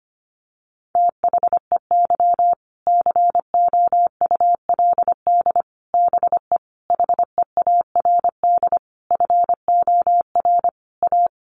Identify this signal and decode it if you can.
Morse code, heard as a keyed tone, about 25 wpm, 700 Hz. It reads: THEY COULD BE HEARD FOR A
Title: Sentences from Top 1200 Words 25wpm